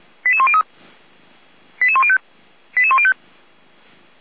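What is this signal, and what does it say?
Five-tone selective call (ZVEI), heard as quick runs of five tones, tones of about 70 ms. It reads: 88103 88105 88105
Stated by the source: ZVEI five-tone selective call 88109 received by SDR.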